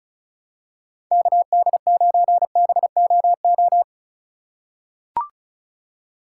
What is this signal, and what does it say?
Morse code, heard as a keyed tone, about 34 wpm, 700 Hz. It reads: KD9BOO
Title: USA Call Signs - 3x Character Spacing - 35wpm